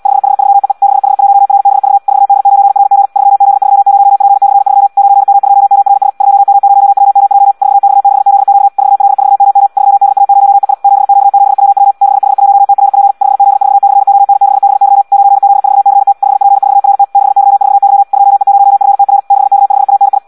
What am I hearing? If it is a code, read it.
Morse code, heard as a keyed tone, about 74 wpm, 800 Hz. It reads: 5H3EE 6H1RBV 4D9RG 3Z30P/3 9M2MRS 2M0WMJ 4X6ZQ 4L4MM 3DA0EI 7Q7WW 6H1TRP 5Z4/ON6BY 9Z4CT 5P4RT 6Y4Y 2E0FTR 6K5ANI
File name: Audio practica CW / qrq-tx-78 2